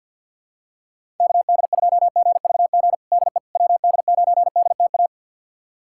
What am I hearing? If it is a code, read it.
Morse code, heard as a keyed tone, about 53 wpm, 700 Hz. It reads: KD2GVG DE WD9DTA
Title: USA Callsign DE Callsign - 6 Characters - 50wpm